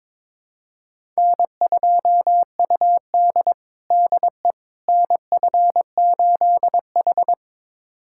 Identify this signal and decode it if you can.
Morse code, heard as a keyed tone, about 22 wpm, 700 Hz. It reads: N2UD DE NF8H
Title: USA Callsign DE Callsign - 4 Characters - 22wpm